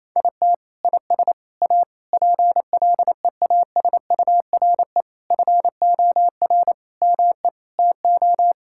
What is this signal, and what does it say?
Morse code, heard as a keyed tone, about 28 wpm, 700 Hz. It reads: IT IS A PLEASURE FOR ME TO
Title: Sentences from Top 1600 Words 28wpm